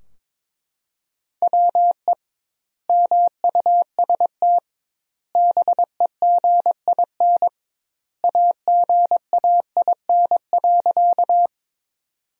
Morse code, heard as a keyed tone, about 22 wpm, 700 Hz. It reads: WE MUST BEGIN AGAIN.
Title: 2x Word-Spacing - Sentences from Top 300 Words - 22wpm